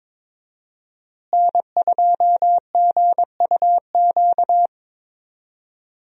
Morse code, heard as a keyed tone, about 22 wpm, 700 Hz. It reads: N2GUQ
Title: USA Call Signs - 5 Characters - Repeated 2x - 22wpm